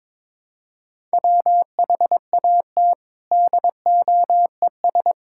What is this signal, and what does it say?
Morse code, heard as a keyed tone, about 22 wpm, 700 Hz. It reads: WHAT DOES
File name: Sets of 9 Words - 22wpm